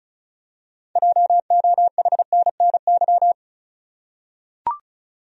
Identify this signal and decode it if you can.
Morse code, heard as a keyed tone, about 36 wpm, 700 Hz. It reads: JOHNNY
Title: USA Common Names - 35wpm